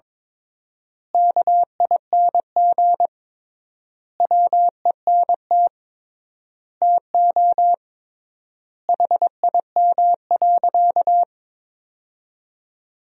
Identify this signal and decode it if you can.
Morse code, heard as a keyed tone, about 21 wpm, 700 Hz. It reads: KING WENT TO HIM.
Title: 3x Word-Spacing - Sentences from Top 400 Words - 22wpm